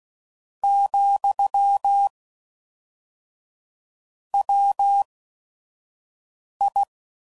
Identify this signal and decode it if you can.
Morse code, heard as a keyed tone, about 16 wpm, spaced as for 5 wpm, 800 Hz. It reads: , WI
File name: CW Code Groups @5WPM / Technical Subjects / XC